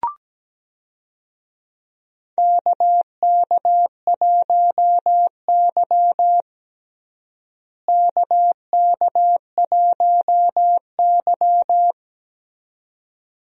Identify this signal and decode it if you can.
Morse code, heard as a keyed tone, about 17 wpm, 700 Hz. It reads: KK1Y KK1Y
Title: USA Call Signs - 4 Characters - Repeated 2x - 17wpm